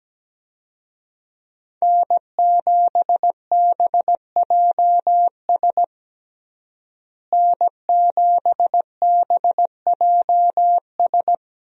Morse code, heard as a keyed tone, about 17 wpm, 700 Hz. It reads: N7BJS N7BJS
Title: USA Call Signs - 5 Characters - Repeated 2x - 17wpm